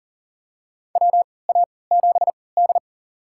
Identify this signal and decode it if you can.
Morse code, heard as a keyed tone, about 40 wpm, spaced as for 23 wpm, 700 Hz. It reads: WA7D
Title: USA Call Signs - 3x Character Spacing - 40wpm